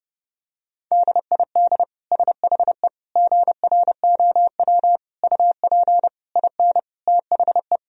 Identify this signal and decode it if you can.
Morse code, heard as a keyed tone, about 30 wpm, 700 Hz. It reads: DID SHE GROW UP IN THE
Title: Sentences from Top 1000 Words - Review Part 2 - 30wpm